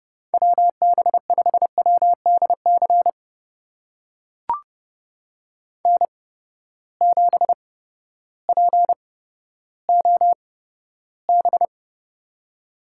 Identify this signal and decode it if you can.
Morse code, heard as a keyed tone, about 30 wpm, spaced as for 14 wpm, 700 Hz. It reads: WB5WDC N7POB
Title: USA Call Signs - 8x Character Spacing - 30wpm